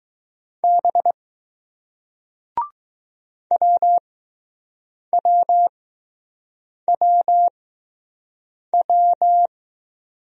Morse code, heard as a keyed tone, about 20 wpm, spaced as for 6 wpm, 700 Hz. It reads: B WWWW